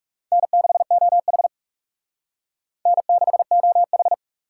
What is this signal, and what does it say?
Morse code, heard as a keyed tone, about 36 wpm, 700 Hz. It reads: N6OH N6OH
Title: Speed Racing - USA 1x2 Call Signs 30wpm